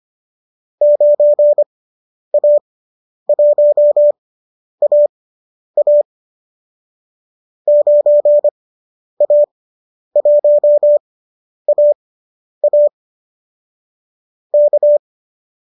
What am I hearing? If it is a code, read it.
Morse code, heard as a keyed tone, about 25 wpm, spaced as for 10 wpm, 590 Hz. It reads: 9A1AA 9A1AA K
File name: qso219 10